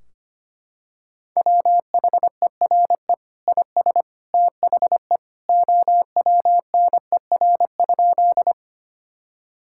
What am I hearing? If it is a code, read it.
Morse code, heard as a keyed tone, about 25 wpm, 700 Hz. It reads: WHERE IS THE OWNER?
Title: Sentences from Top 1600 Words 25wpm